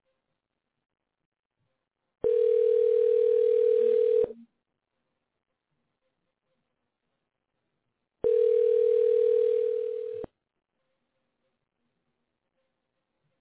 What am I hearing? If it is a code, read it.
Ringback tone (North America), 440 + 480 Hz.